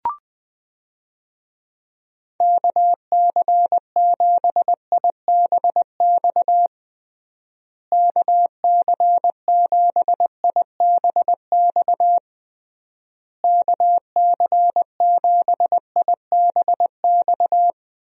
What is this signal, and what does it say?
Morse code, heard as a keyed tone, about 20 wpm, 700 Hz. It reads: KC7IBX KC7IBX KC7IBX
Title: USA Call Signs - 6 Characters - Repeated 3x - 20wpm